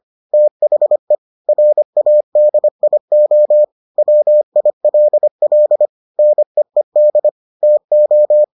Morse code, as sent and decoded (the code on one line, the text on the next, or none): - .... / .-. .- -.. .. --- / .-- .. .-.. .-.. / -. . -.. / - ---
THE RADIO WILL NEED TO